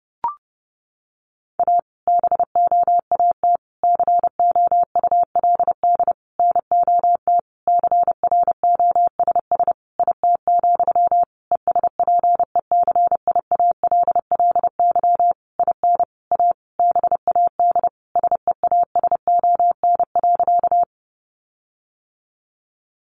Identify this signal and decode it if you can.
Morse code, heard as a keyed tone, about 30 wpm, 700 Hz. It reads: A BOAT COULD NOT CROSS IT, ESPECIALLY IN A BAD SEASON.
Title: Sentences from Top 1400 Words 30wpm